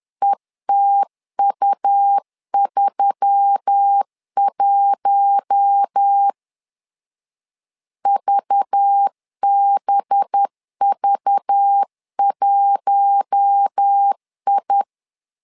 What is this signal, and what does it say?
Morse code, heard as a keyed tone, about 10 wpm, 800 Hz. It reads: ETU31 VBV1I
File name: LEZIONI / 5 LBVC / 5 LBV 10wpm 2248